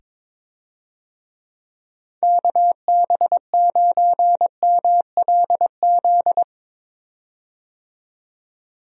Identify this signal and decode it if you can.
Morse code, heard as a keyed tone, about 22 wpm, 700 Hz. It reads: KB9MLZ
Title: USA Call Signs - 6 Characters - 22wpm